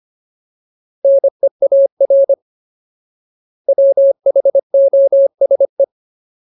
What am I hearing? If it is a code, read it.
Morse code, heard as a keyed tone, about 25 wpm, 550 Hz. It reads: NEAR WHOSE
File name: Ultimate Word List / 25WPM Ultimate